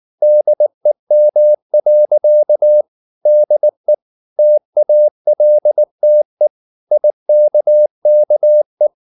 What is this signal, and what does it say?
Morse code, heard as a keyed tone, about 19 wpm, 600 Hz. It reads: DEM. DE TALTE IKKE
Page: Ferie 19 wpm